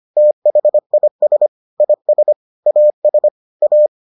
Morse code, heard as a keyed tone, about 25 wpm, 600 Hz. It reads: THIS IS AS A